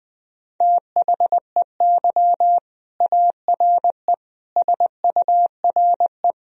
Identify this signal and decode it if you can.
Morse code, heard as a keyed tone, about 20 wpm, 700 Hz. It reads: THEY ARE SURE
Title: Sentences from Top 1000 Words - Review Part 1 - 20wpm